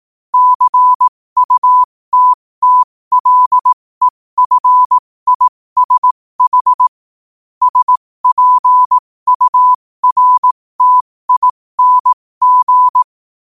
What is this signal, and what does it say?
Morse code, heard as a keyed tone, about 18 wpm, spaced as for 15 wpm, 1000 Hz. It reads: CUTTLEFISH SPURTING
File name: qotdfiles / 15wpm